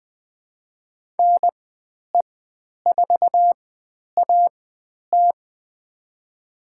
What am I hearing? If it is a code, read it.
Morse code, heard as a keyed tone, about 20 wpm, spaced as for 10 wpm, 700 Hz. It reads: NE4AT
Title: 102 Lesson 17g - Letter u - Call Signs 20wpm